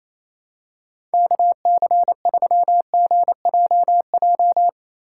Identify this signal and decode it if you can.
Morse code, heard as a keyed tone, about 28 wpm, 700 Hz. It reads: KC3GJJ
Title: USA Call Signs - 6 Characters - 6x Character Spacing - 28wpm